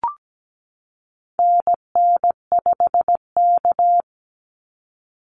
Morse code, heard as a keyed tone, about 17 wpm, 700 Hz. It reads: NN5K
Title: NA SOTA Activator Callsigns 17wpm